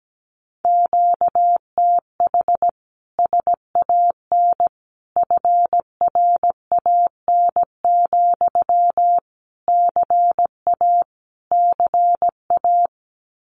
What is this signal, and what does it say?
Morse code, heard as a keyed tone, about 17 wpm, 700 Hz. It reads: QTH SAN FRAN, CA CA
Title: Common QSO Phrases Encore 17wpm